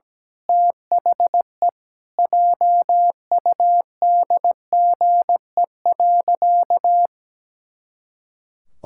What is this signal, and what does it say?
Morse code, heard as a keyed tone, about 17 wpm, 700 Hz. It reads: THE JUDGE.
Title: Pangram Challenge v2 - 17wpm